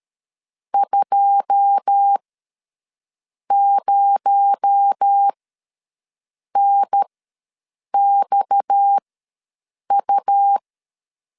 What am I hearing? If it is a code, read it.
Morse code, heard as a keyed tone, about 13 wpm, spaced as for 6 wpm, 800 Hz. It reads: 20NXU
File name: LEZIONI / 9 2 ws 12wpm 3406